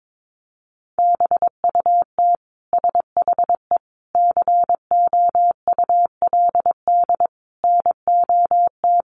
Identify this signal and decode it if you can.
Morse code, heard as a keyed tone, about 22 wpm, 700 Hz. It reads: BUT SHE COULD NOT
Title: Sentences from Top 1200 Words 22wpm